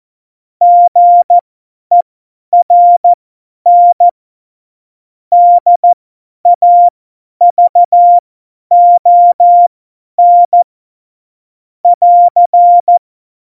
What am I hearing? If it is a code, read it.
Morse code, heard as a keyed tone, about 14 wpm, spaced as for 10 wpm, 700 Hz. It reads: GERN DAVON <AR>
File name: Morsekurs / Witze / Witz 20. Oktober 2020, 50 F70